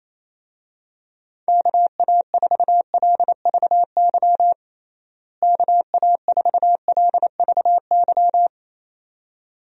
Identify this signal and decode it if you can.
Morse code, heard as a keyed tone, about 28 wpm, 700 Hz. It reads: KA4LVY KA4LVY
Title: USA Call Signs - 6 Characters - Repeated 2x - 28wpm